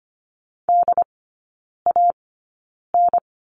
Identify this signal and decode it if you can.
Morse code, heard as a keyed tone, about 25 wpm, spaced as for 9 wpm, 700 Hz. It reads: DAN
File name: course / 25:10wpm / 100 Lesson 17e - Letter u - Three Characters 25wpm